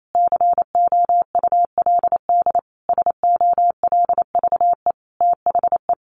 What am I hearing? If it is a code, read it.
Morse code, heard as a keyed tone, about 28 wpm, 700 Hz. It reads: COULD SOLVE THE
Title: Sentences from Top 1000 Words 28wpm